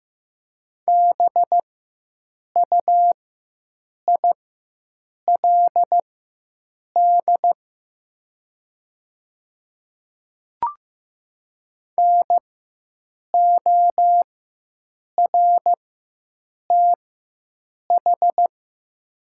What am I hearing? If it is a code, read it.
Morse code, heard as a keyed tone, about 15 wpm, spaced as for 7 wpm, 700 Hz. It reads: BUILD NORTH
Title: 4x Character Spacing - Top 300 Words - 15wpm